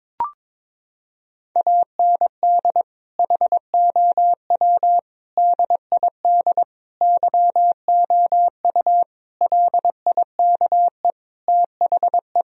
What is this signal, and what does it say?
Morse code, heard as a keyed tone, about 22 wpm, 700 Hz. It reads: AND HOW DID YOU LIKE THE
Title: Sentences from Top 1000 Words - Review Part 4 - 22wpm